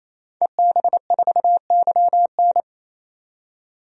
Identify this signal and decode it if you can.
Morse code, heard as a keyed tone, about 28 wpm, 700 Hz. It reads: EB4YN
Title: DXCC-100 Call Signs - 5 Characters - 28wpm